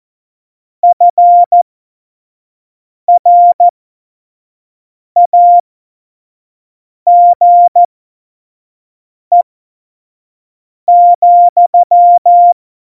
Morse code, heard as a keyed tone, about 14 wpm, spaced as for 5 wpm, 700 Hz. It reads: FRAGE,
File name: Tempo 25 F70